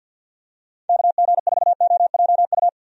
Morse code, heard as a keyed tone, about 51 wpm, 700 Hz. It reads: KG4OJU
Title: USA Call Signs - 6 Characters - 50wpm